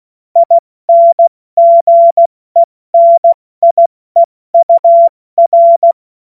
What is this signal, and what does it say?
Morse code, heard as a keyed tone, about 16 wpm, spaced as for 14 wpm, 680 Hz. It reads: INGENIEUR
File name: Morsekurs / Witze / Witz 07. Dezember 2021, 70 F80